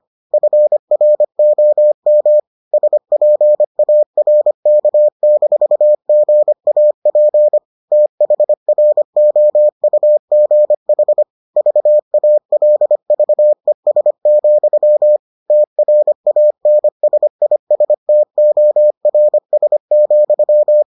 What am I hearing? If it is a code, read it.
Morse code, heard as a keyed tone, about 25 wpm, 600 Hz. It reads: FROM SPARK-GAP THROUGH VALVES, TRANSISTORS,